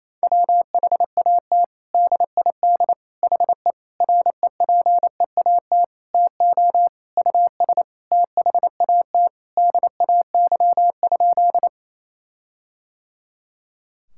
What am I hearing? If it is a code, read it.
Morse code, heard as a keyed tone, about 28 wpm, 700 Hz. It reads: WHAT DID HE REPEAT TO US THAT DAY?
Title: Sentences from Top 1600 Words 28wpm